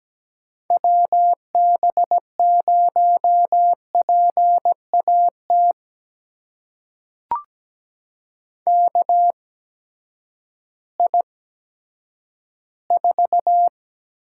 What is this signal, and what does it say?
Morse code, heard as a keyed tone, about 17 wpm, spaced as for 10 wpm, 700 Hz. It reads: WB0PAT KI4
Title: USA Call Signs - 6 Characters - 8x Character Spacing - 17wpm